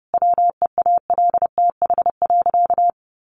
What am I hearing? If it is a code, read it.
Morse code, heard as a keyed tone, about 30 wpm, 700 Hz. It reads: WEALTH.
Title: Sentences from Top 1200 Words 30wpm